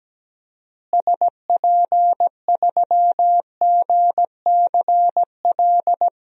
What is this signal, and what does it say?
Morse code, heard as a keyed tone, about 17 wpm, 700 Hz. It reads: SP3GCL
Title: DXCC-100 Call Signs - 6 Characters - Repeated 2x - 17wpm